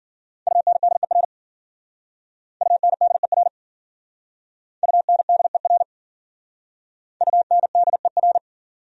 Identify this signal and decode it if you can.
Morse code, heard as a keyed tone, about 52 wpm, 700 Hz. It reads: UNDER UNDER UNDER UNDER